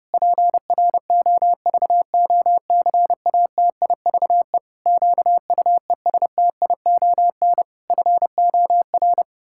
Morse code, heard as a keyed tone, about 28 wpm, 700 Hz. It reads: PROVOCATIVE QUESTION FOR
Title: Pangram Challenge v2 - 30wpm